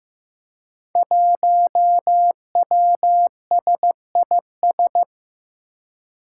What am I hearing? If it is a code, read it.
Morse code, heard as a keyed tone, about 15 wpm, 700 Hz. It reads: 1WSIS
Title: Five Challenging Letter-Number Combinations 15wpm